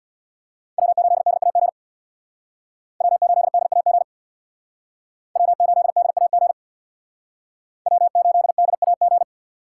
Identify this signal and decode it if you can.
Morse code, heard as a keyed tone, about 62 wpm, 700 Hz. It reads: W8DAG W8DAG W8DAG W8DAG
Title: Speed Racing - USA 1x3 Call Signs 50wpm